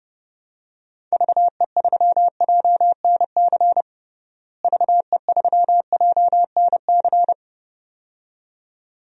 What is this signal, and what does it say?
Morse code, heard as a keyed tone, about 30 wpm, 700 Hz. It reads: VE3JNC VE3JNC
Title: DXCC-100 Call Signs - 6 Characters - Repeated 2x - 30wpm